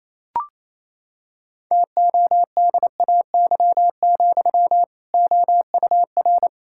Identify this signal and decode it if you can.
Morse code, heard as a keyed tone, about 28 wpm, 700 Hz. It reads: TODAY, OUR
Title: Sentences from Top 1600 Words 28wpm